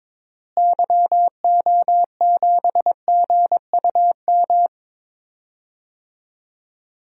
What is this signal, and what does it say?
Morse code, heard as a keyed tone, about 22 wpm, 700 Hz. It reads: YO7GUM
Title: DXCC-100 Call Signs - 6 Characters - Repeated 2x - 22wpm